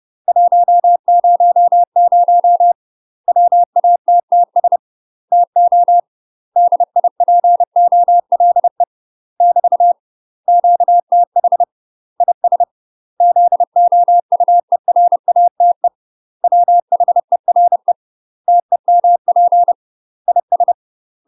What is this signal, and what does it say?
Morse code, heard as a keyed tone, about 30 wpm, 700 Hz. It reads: 100 WATTS TO DIPOLE <BT> QTH IS ZOUERATE WHERE TEMP IS
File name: UK MASTER QSOs TRAINING / 30WPM / UK MASTER QSOs 0024